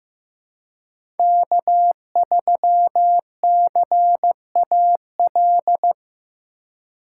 Morse code, heard as a keyed tone, about 15 wpm, 700 Hz. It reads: K3CAL
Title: USA Call Signs - 5 Characters - 3x Character Spacing - 15wpm